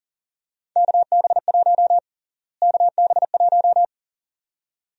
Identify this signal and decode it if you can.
Morse code, heard as a keyed tone, about 40 wpm, 700 Hz. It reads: KB1 KB1
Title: USA Call Sign Prefixes - Repeated 2x - 40wpm